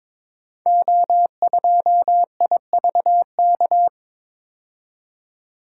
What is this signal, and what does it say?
Morse code, heard as a keyed tone, about 22 wpm, 700 Hz. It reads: O2IVK
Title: Code Groups - 5 Letter:Numbers - 22wpm